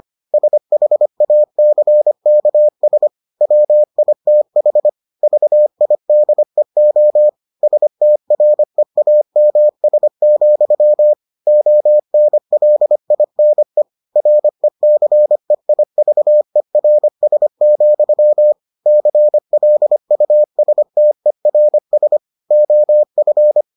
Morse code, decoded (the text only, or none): SHACKS WITH VIDEO STREAMS, ONLINE RECEIVERS, CLUSTERS OF